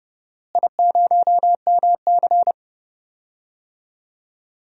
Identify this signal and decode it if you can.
Morse code, heard as a keyed tone, about 31 wpm, 700 Hz. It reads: I0MC
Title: DXCC-100 Call Signs - 4 Characters - Repeated 3x - 30wpm